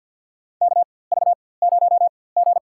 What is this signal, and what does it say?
Morse code, heard as a keyed tone, about 47 wpm, spaced as for 23 wpm, 700 Hz. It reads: KV0G
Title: USA Call Signs - 4x Character Spacing - 50wpm